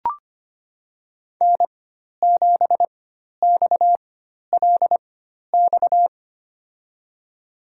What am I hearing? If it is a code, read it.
Morse code, heard as a keyed tone, about 25 wpm, spaced as for 12 wpm, 700 Hz. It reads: N7XLX
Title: USA Call Signs - 5 Characters - 4x Character Spacing - 25wpm